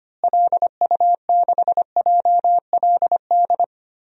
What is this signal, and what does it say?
Morse code, heard as a keyed tone, about 25 wpm, 700 Hz. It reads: LU6JLD
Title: DXCC-100 Call Signs - 6 Characters - Repeated 3x - 25wpm